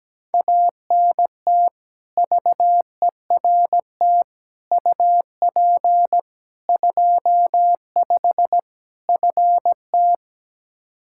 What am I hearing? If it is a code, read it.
Morse code, heard as a keyed tone, about 17 wpm, 700 Hz. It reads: ANT VERT UP 25 FT
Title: Practice QSO - Exchange 2 - Info - 17wpm